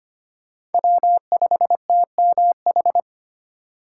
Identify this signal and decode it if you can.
Morse code, heard as a keyed tone, about 31 wpm, 700 Hz. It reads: W5TMH
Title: Speed Racing - USA Call Signs 25wpm